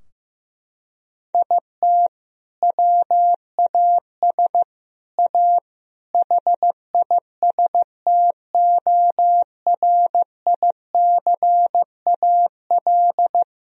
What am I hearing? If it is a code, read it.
Morse code, heard as a keyed tone, about 15 wpm, 700 Hz. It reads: IT WAS A HISTORICAL
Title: Sentences from Top 1000 Words - Review Part 4 - 15wpm